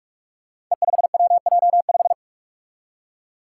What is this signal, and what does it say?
Morse code, heard as a keyed tone, about 46 wpm, 700 Hz. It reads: E5WJ5
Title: Five Challenging Letter-Number Combinations 45wpm